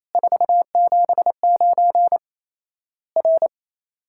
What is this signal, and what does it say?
Morse code, heard as a keyed tone, about 27 wpm, 700 Hz. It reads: 479 R
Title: Worked All Europe DX Contest - QTC Exchange - 28wpm